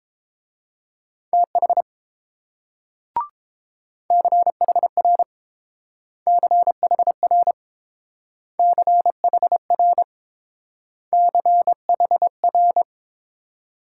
Morse code, heard as a keyed tone, about 29 wpm, 700 Hz. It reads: TH CHR CHR CHR CHR